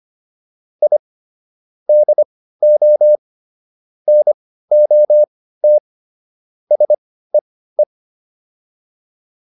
Morse code, heard as a keyed tone, about 25 wpm, spaced as for 15 wpm, 600 Hz. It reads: I DO NOT SEE
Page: intermediate-practice-files